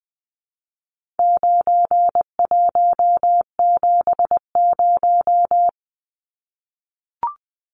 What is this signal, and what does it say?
Morse code, heard as a keyed tone, about 20 wpm, 700 Hz. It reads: 9170
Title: Numbers - 4-Digit 20wpm